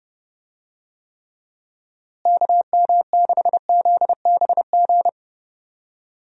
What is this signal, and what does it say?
Morse code, heard as a keyed tone, about 30 wpm, 700 Hz. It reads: KM6ZBG
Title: USA Call Signs - 6 Characters - Repeated 2x - 30wpm